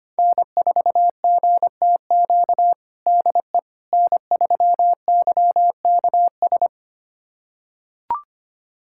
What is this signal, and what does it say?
Morse code, heard as a keyed tone, about 25 wpm, 700 Hz. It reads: N4GTQ DE N3YKS
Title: USA Callsign DE Callsign - 5 Characters - 25wpm